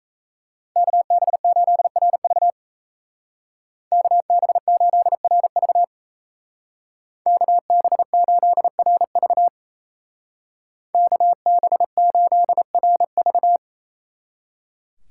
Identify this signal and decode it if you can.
Morse code, heard as a keyed tone, about 35 wpm, 700 Hz. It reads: KB8RV KB8RV KB8RV KB8RV
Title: Speed Racing - USA 2x2 Call Signs 28wpm